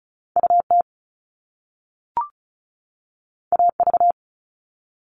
Morse code, heard as a keyed tone, about 36 wpm, 700 Hz. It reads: UT AV
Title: Top Two Letter Combinations in Words 35wpm